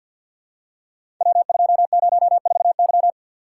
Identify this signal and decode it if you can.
Morse code, heard as a keyed tone, about 51 wpm, 700 Hz. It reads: W204Y
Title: Code Groups - 5 Letter:Numbers - 50wpm